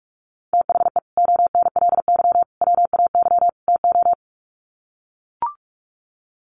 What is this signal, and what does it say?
Morse code, heard as a keyed tone, about 44 wpm, 700 Hz. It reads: THE ONLY WAY TO
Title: Sets of 4 Words 45wpm